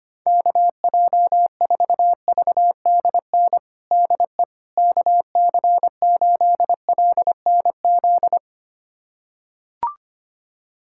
Morse code, heard as a keyed tone, about 25 wpm, 700 Hz. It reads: KJ4VDN DE KC8LNZ
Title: USA Callsign DE Callsign - 6 Characters - 25wpm